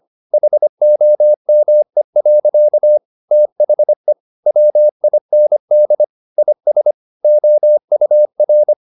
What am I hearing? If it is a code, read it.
Morse code, heard as a keyed tone, about 25 wpm, 600 Hz. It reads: HOME. THE WIND IS OUR